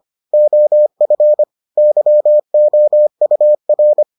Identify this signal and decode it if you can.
Morse code, heard as a keyed tone, about 25 wpm, 600 Hz. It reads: OF YOUR